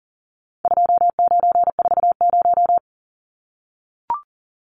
Morse code, heard as a keyed tone, about 40 wpm, 700 Hz. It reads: 2940
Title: Numbers - 4-Digit 40wpm